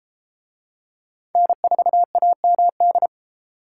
Morse code, heard as a keyed tone, about 31 wpm, 700 Hz. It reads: N4AMD